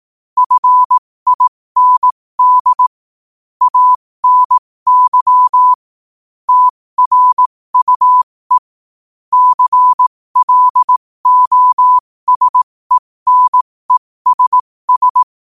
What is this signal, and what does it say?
Morse code, heard as a keyed tone, about 18 wpm, spaced as for 15 wpm, 1000 Hz. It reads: FIND ANY TRUE CLOSENESS
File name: qotdfiles / 15wpm